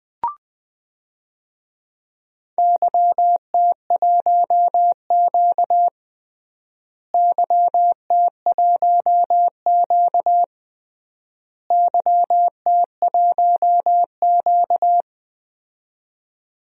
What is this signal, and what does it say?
Morse code, heard as a keyed tone, about 21 wpm, 700 Hz. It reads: YT1Q YT1Q YT1Q
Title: DXCC-100 Call Signs - 4 Characters - Repeated 3x - 20wpm